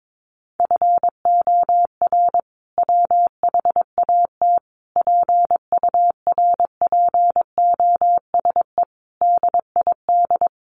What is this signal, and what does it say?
Morse code, heard as a keyed tone, about 22 wpm, 700 Hz. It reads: FOR WHAT PURPOSE DID
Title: Sentences from Top 800 Words 22wpm